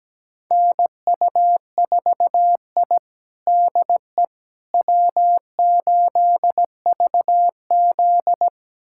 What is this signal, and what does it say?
Morse code, heard as a keyed tone, about 17 wpm, 700 Hz. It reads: NU4I DE W8VZ
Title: USA Callsign DE Callsign - 4 Characters - 17wpm